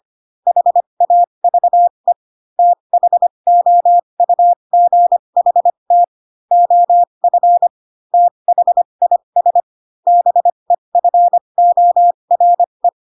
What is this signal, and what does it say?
Morse code, heard as a keyed tone, about 25 wpm, spaced as for 22 wpm, 700 Hz. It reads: HAVE THOUGHT OF THIS BEFORE
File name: Pinocchio / Pinocchio ch3 18 25 ik1hge